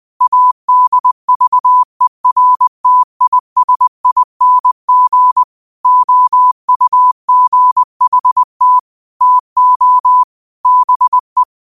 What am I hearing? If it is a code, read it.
Morse code, heard as a keyed tone, about 20 wpm, 1000 Hz. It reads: ADVERTISING OUGHT TO BE